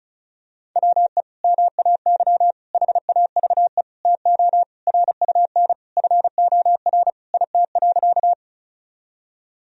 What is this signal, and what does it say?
Morse code, heard as a keyed tone, about 36 wpm, 700 Hz. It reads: WE MAY HAVE TO RUN FOR IT.
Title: Sentences from Top 1000 Words - Review Part 1 - 35wpm